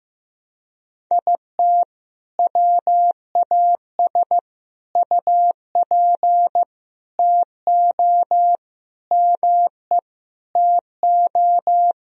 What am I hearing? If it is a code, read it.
Morse code, heard as a keyed tone, about 16 wpm, 700 Hz. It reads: IT WAS UP TO ME TO
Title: Sentences from Top 1000 Words - Review Part 2 - 15wpm